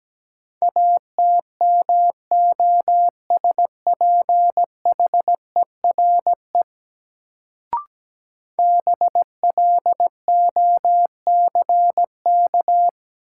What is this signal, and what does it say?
Morse code, heard as a keyed tone, about 17 wpm, 700 Hz. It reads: ATMOSPHERE BLOCK
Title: Sentences from Top 1600 Words - New Word Drill - 17wpm